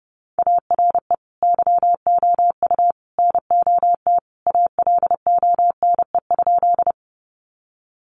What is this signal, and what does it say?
Morse code, heard as a keyed tone, about 30 wpm, 700 Hz. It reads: ARE YOU NOT ALONE?
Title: Sentences from Top 1000 Words 30wpm